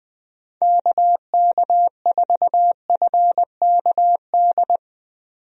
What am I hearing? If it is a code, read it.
Morse code, heard as a keyed tone, about 20 wpm, 700 Hz. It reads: KK4FKD
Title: USA Call Signs - 6 Characters - 20wpm